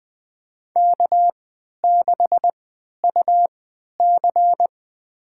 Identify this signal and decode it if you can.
Morse code, heard as a keyed tone, about 20 wpm, spaced as for 11 wpm, 700 Hz. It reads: K6UC
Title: USA Call Signs - 4 Characters - 3x Character Spacing - 20wpm